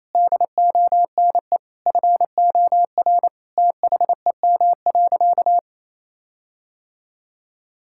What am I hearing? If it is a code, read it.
Morse code, heard as a keyed tone, about 28 wpm, 700 Hz. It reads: DONE FOR THEM.
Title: Sentences from Top 1600 Words 28wpm